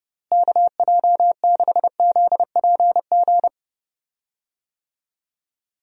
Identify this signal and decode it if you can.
Morse code, heard as a keyed tone, about 30 wpm, 700 Hz. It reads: KJ6ZPG
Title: USA Call Signs - 6 Characters - Repeated 2x - 30wpm